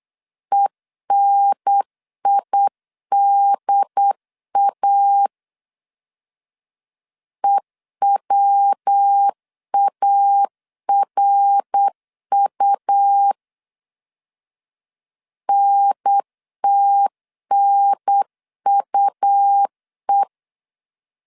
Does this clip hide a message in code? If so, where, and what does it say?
Yes: Morse code from 0.5 to 20.2 s, heard as a keyed tone, about 8 wpm, 800 Hz. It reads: ENIDA EWARU NTNUE